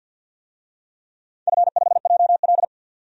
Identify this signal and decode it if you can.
Morse code, heard as a keyed tone, about 47 wpm, 700 Hz. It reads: F5JL
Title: DXCC-100 Call Signs - 4 Characters - 50wpm